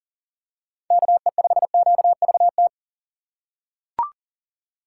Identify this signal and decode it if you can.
Morse code, heard as a keyed tone, about 39 wpm, 690 Hz. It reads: KE5QVT
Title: USA Call Signs - 6x Character Spacing - 40wpm